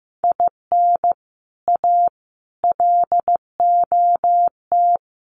Morse code, heard as a keyed tone, about 15 wpm, 700 Hz. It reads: IN A LOT
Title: Sets of 4 Words 15wpm